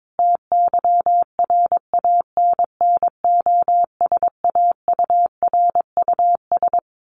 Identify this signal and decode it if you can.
Morse code, heard as a keyed tone, about 22 wpm, 700 Hz. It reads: TYRANNOSAURUS
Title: A Sound Of Thunder 22wpm